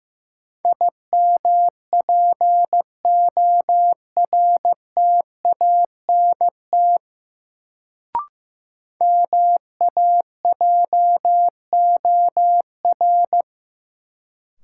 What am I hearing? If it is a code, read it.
Morse code, heard as a keyed tone, about 15 wpm, 700 Hz. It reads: IMPORTANT MAJOR
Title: Sentences from Top 600 Words - New Word Drill - 15wpm